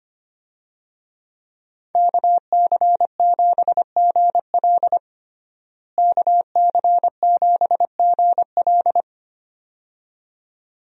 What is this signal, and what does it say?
Morse code, heard as a keyed tone, about 25 wpm, 700 Hz. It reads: KC7GL KC7GL
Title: USA Call Signs - 5 Characters - Repeated 2x - 25wpm